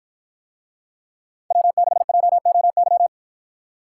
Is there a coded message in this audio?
Yes: Morse code from 1.5 to 3.1 s, heard as a keyed tone, about 53 wpm, 700 Hz. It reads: W6JOY